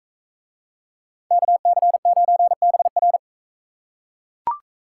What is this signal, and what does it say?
Morse code, heard as a keyed tone, about 37 wpm, 700 Hz. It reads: KC9BR